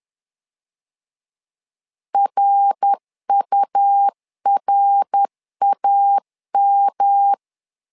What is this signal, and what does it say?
Morse code, heard as a keyed tone, about 10 wpm, 800 Hz. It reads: RURAM
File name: LEZIONI / 3 MWRO / 3 MWR 10wpm 1952